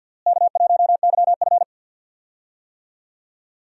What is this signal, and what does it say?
Morse code, heard as a keyed tone, about 52 wpm, 700 Hz. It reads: K1YF
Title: USA Call Signs - 4 Characters - Repeated 3x - 50wpm